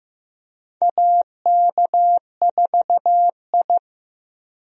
Morse code, heard as a keyed tone, about 15 wpm, 700 Hz. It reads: AK4I
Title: USA Call Signs - 4 Characters - 15wpm